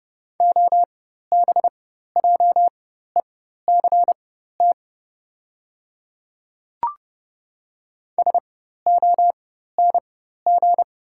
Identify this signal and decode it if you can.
Morse code, heard as a keyed tone, about 30 wpm, spaced as for 14 wpm, 700 Hz. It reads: OBJECT SONG
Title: 4x Character Spacing - Top 500 Words - 30wpm